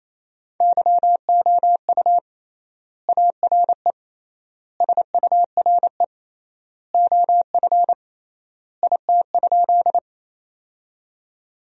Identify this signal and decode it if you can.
Morse code, heard as a keyed tone, about 28 wpm, 700 Hz. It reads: YOU ARE SURE OF IT?
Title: 3x Word-Spacing - Sentences from Top 300 Words - 28wpm